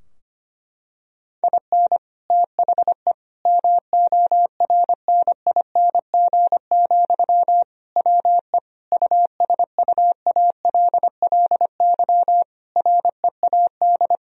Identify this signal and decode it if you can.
Morse code, heard as a keyed tone, about 25 wpm, 700 Hz. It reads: IN THE MORNING, WE USUALLY READ
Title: Sentences from Top 1000 Words - Review Part 3 - 25wpm